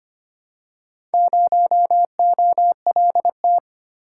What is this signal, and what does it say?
Morse code, heard as a keyed tone, about 25 wpm, 700 Hz. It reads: 0OLT
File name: practice / 25wpm / Four Letter-Number Combinations - 25wpm